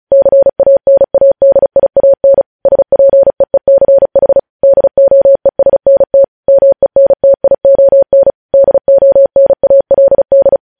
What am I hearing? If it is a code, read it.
Morse code, heard as a keyed tone, about 36 wpm, 550 Hz. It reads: CANADIAN SPEECH DOESNT MENTION DONALD
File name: podcast / News headlines-1x-35x35